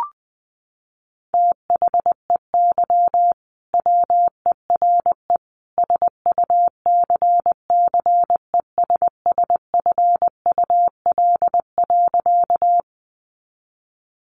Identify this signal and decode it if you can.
Morse code, heard as a keyed tone, about 19 wpm, 700 Hz. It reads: THEY WERE SUCCESSFUL.
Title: Sentences from Top 900 Words 20wpm